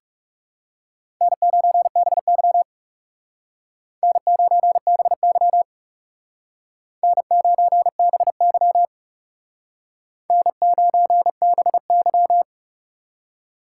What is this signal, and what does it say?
Morse code, heard as a keyed tone, about 38 wpm, 700 Hz. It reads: N9BY N9BY N9BY N9BY